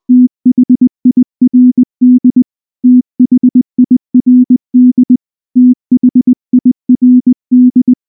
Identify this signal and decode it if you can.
Morse code, heard as a keyed tone, about 20 wpm, 250 Hz. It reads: THIRD THIRD THIRD
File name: morse / morse-training-20WPM-9